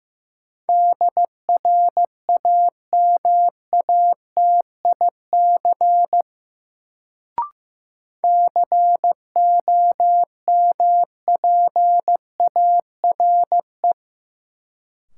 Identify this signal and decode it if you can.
Morse code, heard as a keyed tone, about 15 wpm, 700 Hz. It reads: DRAMATIC COMPARE
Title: Sentences from Top 1300 Words - New Word Drill - 15wpm